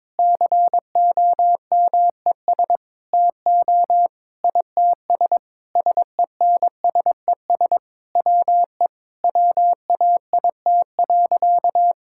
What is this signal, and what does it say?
Morse code, heard as a keyed tone, about 22 wpm, 700 Hz. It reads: COMES TO ITS SENSES WE WAIT.
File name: Sentences with 9-Word Phrases - 22wpm